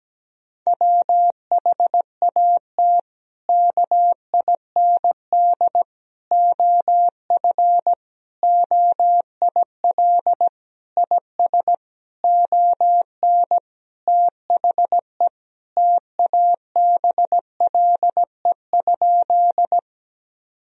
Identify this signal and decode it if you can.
Morse code, heard as a keyed tone, about 17 wpm, 700 Hz. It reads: WHAT KIND OF OIL IS ON THE TABLE?
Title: Sentences from Top 1000 Words - Review Part 4 - 17wpm